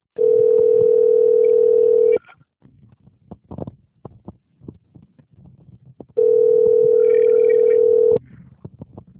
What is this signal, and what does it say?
Ringback tone (North America), 440 + 480 Hz.
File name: rg-1600-unknown- - - .345